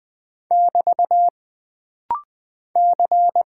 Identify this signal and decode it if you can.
Morse code, heard as a keyed tone, about 20 wpm, 700 Hz. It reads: <BT> C
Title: Single Character (Letter:Number:Punctuation:Prosign) - Rapid-Fire - 20wpm